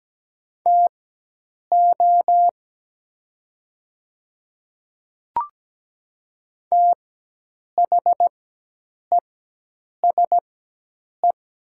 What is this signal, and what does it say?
Morse code, heard as a keyed tone, about 17 wpm, spaced as for 8 wpm, 700 Hz. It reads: TO THESE
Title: 4x Character Spacing - Top 100 Words - 17wpm